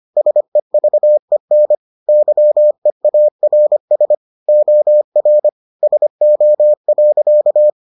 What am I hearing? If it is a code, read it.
Morse code, heard as a keyed tone, about 25 wpm, 600 Hz. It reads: SEVEN YEARS OR SO.